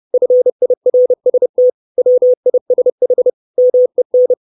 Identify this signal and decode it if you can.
Morse code, heard as a keyed tone, about 30 wpm, 500 Hz. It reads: FIRST WISH MEN
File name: 632 Words 30WPM